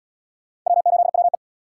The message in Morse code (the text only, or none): W8WE